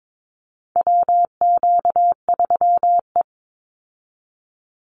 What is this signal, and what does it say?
Morse code, heard as a keyed tone, about 22 wpm, 700 Hz. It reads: WQ3E
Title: Call Signs Repeated 2x - 22wpm